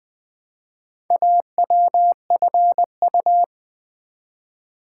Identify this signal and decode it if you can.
Morse code, heard as a keyed tone, about 20 wpm, 690 Hz. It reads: AWFU
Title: Build A Word - 1 Letter At A Time - 5-Letter Words - 20wpm